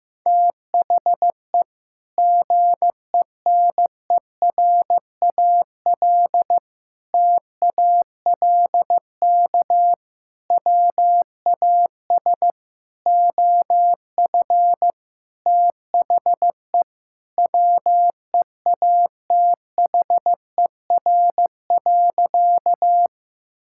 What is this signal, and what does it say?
Morse code, heard as a keyed tone, about 16 wpm, 700 Hz. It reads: THE GENERAL TALK WAS OF THE WEATHER.
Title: Sentences from Top 1000 Words - Review Part 5 - 15wpm